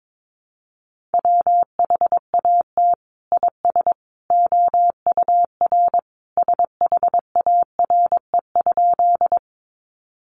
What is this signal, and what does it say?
Morse code, heard as a keyed tone, about 22 wpm, 700 Hz. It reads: WHAT IS OUR SHARE?
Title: Sentences from Top 900 Words 22wpm